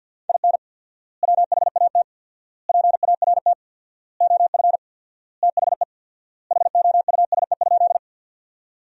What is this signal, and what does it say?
Morse code, heard as a keyed tone, about 52 wpm, 700 Hz. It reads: IN WHAT PART OF THE HOUSE?
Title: 4x Word-Spacing - Sentences from Top 200 Words - 50wpm